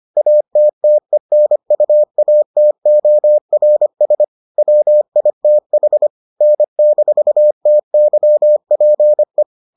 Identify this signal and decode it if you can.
Morse code, heard as a keyed tone, about 25 wpm, 600 Hz. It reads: ATTENUATORS WITH N-TYPE